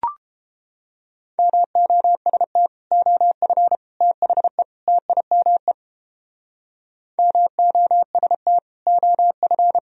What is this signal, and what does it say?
Morse code, heard as a keyed tone, about 29 wpm, 700 Hz. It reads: MOST OF THE TIME MOST OF